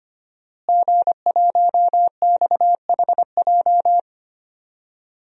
Morse code, heard as a keyed tone, about 25 wpm, 700 Hz. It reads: G1XHJ
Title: DXCC-100 Call Signs - 5 Characters - Repeated 3x - 25wpm